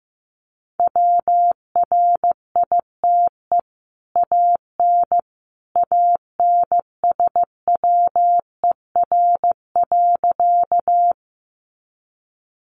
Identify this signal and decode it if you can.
Morse code, heard as a keyed tone, about 15 wpm, 700 Hz. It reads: WRITE AN ANSWER.
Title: Sentences from Top 300 Words 15wpm